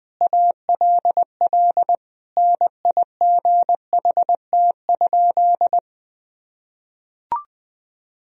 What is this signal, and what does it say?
Morse code, heard as a keyed tone, about 20 wpm, 700 Hz. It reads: ALL NIGHT?
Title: Sentences from Top 1000 Words - Review Part 4 - 20wpm